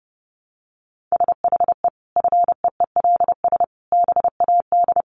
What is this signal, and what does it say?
Morse code, heard as a keyed tone, about 30 wpm, 700 Hz. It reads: SHE FEELS BAD
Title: Additional Easy Phrases 30wpm